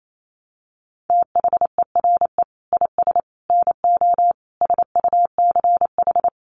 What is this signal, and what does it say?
Morse code, heard as a keyed tone, about 28 wpm, 700 Hz. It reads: THERE IS NO SUCH
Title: Sets of 5 Words Encore 28wpm